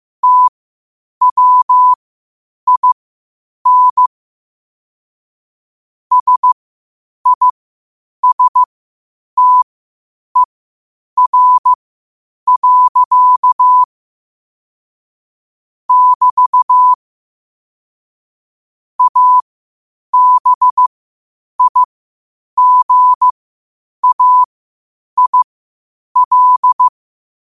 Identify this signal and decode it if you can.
Morse code, heard as a keyed tone, about 15 wpm, spaced as for 8 wpm, 1000 Hz. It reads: TWIN SISTER. <BT> ABIGAIL